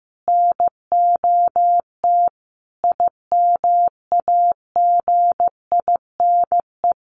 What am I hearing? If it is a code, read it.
Morse code, heard as a keyed tone, about 14 wpm, 700 Hz. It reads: NOT IMAGINE
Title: Sentences from Top 1400 Words 15wpm